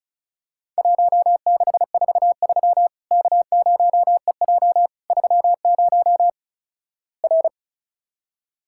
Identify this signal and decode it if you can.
Morse code, heard as a keyed tone, about 36 wpm, 700 Hz. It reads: 1643 K0EJ 30 R
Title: Worked All Europe DX Contest - QTC Exchange - 35wpm